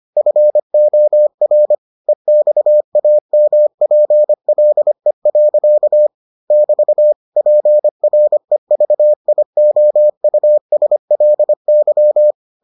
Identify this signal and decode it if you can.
Morse code, heard as a keyed tone, about 25 wpm, 600 Hz. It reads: FOR EXAMPLE. <BT> PREVIOUSLY